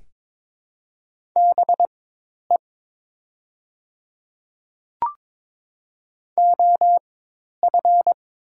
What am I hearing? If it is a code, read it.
Morse code, heard as a keyed tone, about 22 wpm, spaced as for 10 wpm, 700 Hz. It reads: BE OF
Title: 4x Character Spacing - Top 100 Words - 22wpm